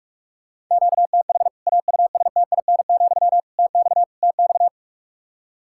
Morse code, heard as a keyed tone, about 41 wpm, 700 Hz. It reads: QTH AUSTIN, TX TX
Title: K1USN SST QSO Practice 45wpm